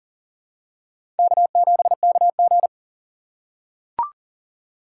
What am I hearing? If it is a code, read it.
Morse code, heard as a keyed tone, about 40 wpm, 700 Hz. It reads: K7KG
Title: USA Call Signs - 4x Character Spacing - 40wpm